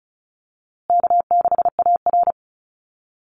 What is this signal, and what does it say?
Morse code, heard as a keyed tone, about 34 wpm, 700 Hz. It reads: K6AR
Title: Easy Call Signs Repeated 2x - 35wpm